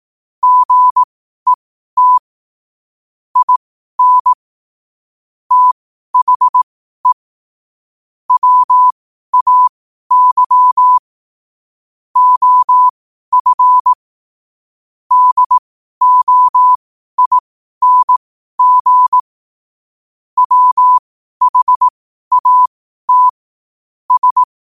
Morse code, heard as a keyed tone, about 18 wpm, spaced as for 12 wpm, 1000 Hz. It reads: GET IN THE WAY OF DOING WHAT S
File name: qotdfiles / 12wpm